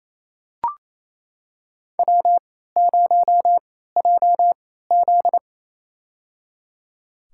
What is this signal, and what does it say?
Morse code, heard as a keyed tone, about 28 wpm, spaced as for 16 wpm, 700 Hz. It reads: W0JZ
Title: USA Call Signs - 4 Characters - 3x Character Spacing - 28wpm